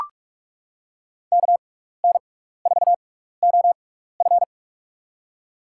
Morse code, heard as a keyed tone, about 43 wpm, spaced as for 15 wpm, 700 Hz. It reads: KN4OF
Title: USA Call Signs - 5 Characters - 6x Character Spacing - 45wpm